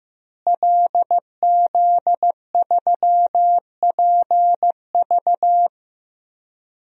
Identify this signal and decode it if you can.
Morse code, heard as a keyed tone, about 15 wpm, 700 Hz. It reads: LZ3PV
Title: DXCC-100 Call Signs - 5 Characters - Repeated 2x - 15wpm